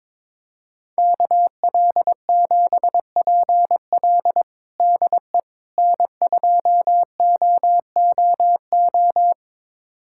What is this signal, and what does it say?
Morse code, heard as a keyed tone, about 22 wpm, 700 Hz. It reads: KL7PL DE N2OOO
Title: USA Callsign DE Callsign - 5 Characters - 22wpm